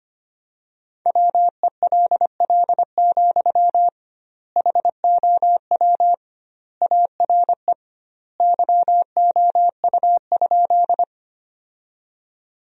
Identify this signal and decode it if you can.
Morse code, heard as a keyed tone, about 25 wpm, 700 Hz. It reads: WELL, HOW ARE YOU?
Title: 2x Word-Spacing - Sentences from Top 200 Words - 25wpm